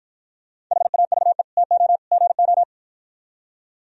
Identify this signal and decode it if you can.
Morse code, heard as a keyed tone, about 52 wpm, 700 Hz. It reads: HAVE TO GO